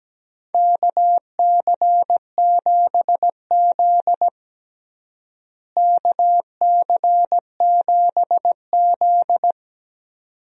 Morse code, heard as a keyed tone, about 17 wpm, 700 Hz. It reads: KC7Z KC7Z
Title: USA Call Signs - 4 Characters - Repeated 2x - 17wpm